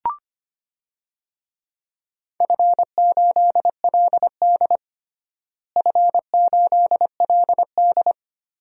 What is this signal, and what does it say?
Morse code, heard as a keyed tone, about 25 wpm, 700 Hz. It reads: F8LD F8LD
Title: DXCC-100 Call Signs - 4 Characters - Repeated 2x - 25wpm